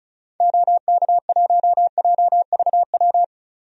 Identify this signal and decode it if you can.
Morse code, heard as a keyed tone, about 34 wpm, 700 Hz. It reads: OK1JVW
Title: DXCC-100 Call Signs - 6 Characters - 35wpm